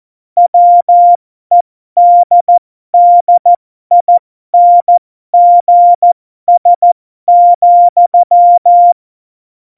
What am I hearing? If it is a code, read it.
Morse code, heard as a keyed tone, about 14 wpm, spaced as for 12 wpm, 700 Hz. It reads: WEDDINGS,
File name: Morsekurs / Witze